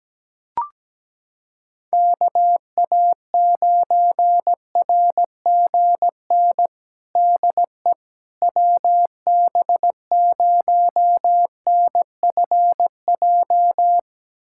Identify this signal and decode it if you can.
Morse code, heard as a keyed tone, about 17 wpm, 700 Hz. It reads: KA9RGN DE WB0NFJ
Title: USA Callsign DE Callsign - 6 Characters - 17wpm